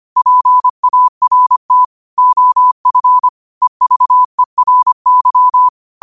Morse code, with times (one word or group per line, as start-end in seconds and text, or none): 0.2-1.8 PART
2.2-3.3 OF
3.6-5.7 EVERY